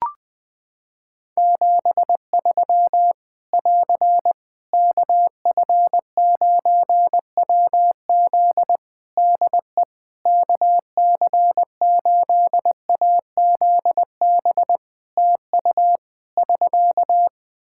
Morse code, heard as a keyed tone, about 20 wpm, 700 Hz. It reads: 73 <AR> KF9WZ DE KC8AZB TU <SK>
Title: Practice QSO - Exchange 4 - Ending - 20wpm